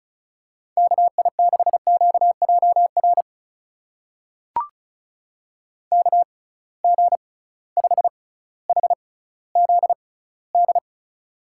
Morse code, heard as a keyed tone, about 36 wpm, spaced as for 18 wpm, 700 Hz. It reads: KI6QJR KG5HZD
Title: USA Call Signs - 6 Characters - 6x Character Spacing - 35wpm